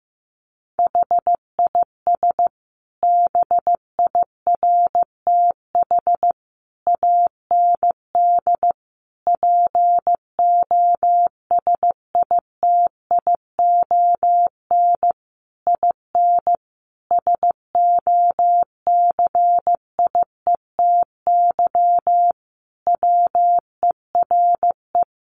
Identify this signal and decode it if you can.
Morse code, heard as a keyed tone, about 15 wpm, 700 Hz. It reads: HIS BIRTH AND POSITION IN SOCIETY WERE
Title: Sentences from Top 900 Words 15wpm